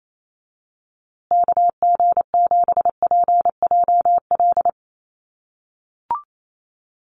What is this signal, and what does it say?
Morse code, heard as a keyed tone, about 28 wpm, 700 Hz. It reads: KG7PJL
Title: USA Call Signs Repeated 3x 28wpm